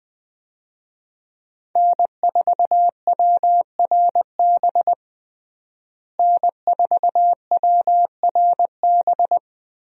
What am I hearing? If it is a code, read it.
Morse code, heard as a keyed tone, about 21 wpm, 700 Hz. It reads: N4WRB N4WRB
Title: USA Call Signs - 5 Characters - Repeated 3x - 20wpm